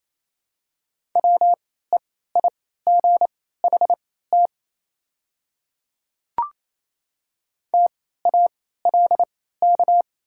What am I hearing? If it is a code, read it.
Morse code, heard as a keyed tone, about 28 wpm, spaced as for 16 wpm, 700 Hz. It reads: WEIGHT TALK
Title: 3x Character Spacing - Top 500 Words - 28wpm